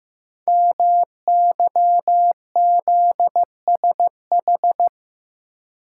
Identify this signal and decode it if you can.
Morse code, heard as a keyed tone, about 15 wpm, 700 Hz. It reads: MYZSH
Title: Code Groups - 5 Letter:Numbers - 15wpm